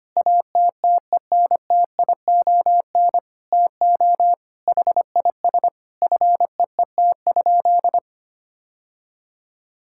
Morse code, heard as a keyed tone, about 25 wpm, 700 Hz. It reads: ATTENTION TO HIS FEET?
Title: Sentences from Top 1000 Words - Review Part 6 - 25wpm